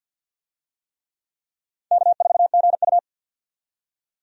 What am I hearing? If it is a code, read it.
Morse code, heard as a keyed tone, about 50 wpm, 700 Hz. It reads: K4GU
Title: USA Call Signs - 4 Characters - 50wpm